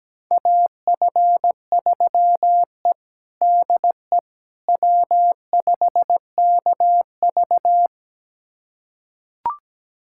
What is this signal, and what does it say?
Morse code, heard as a keyed tone, about 17 wpm, 700 Hz. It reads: AF3E DE W5KV
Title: USA Callsign DE Callsign - 4 Characters - 17wpm